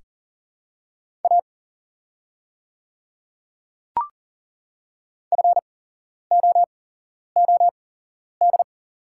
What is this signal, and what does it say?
Morse code, heard as a keyed tone, about 37 wpm, spaced as for 11 wpm, 700 Hz. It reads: A FOOD
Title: 8x Character Spacing - Top 300 Words - 40wpm